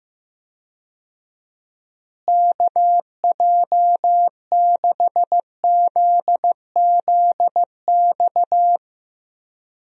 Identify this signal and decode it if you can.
Morse code, heard as a keyed tone, about 15 wpm, 700 Hz. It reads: KJ6ZZX
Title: USA Call Signs - 6 Characters - Repeated 2x - 15wpm